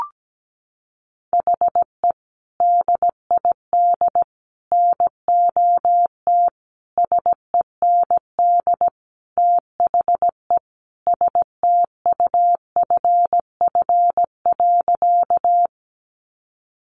Morse code, heard as a keyed tone, about 17 wpm, 700 Hz. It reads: HE DID NOT SEND THE STUFF.
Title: Sentences from Top 1500 Words 17wpm